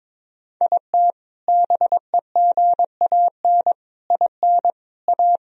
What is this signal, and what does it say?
Morse code, heard as a keyed tone, about 22 wpm, 700 Hz. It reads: IT BEGAN IN A
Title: Sentences from Top 1000 Words - Review Part 1 - 22wpm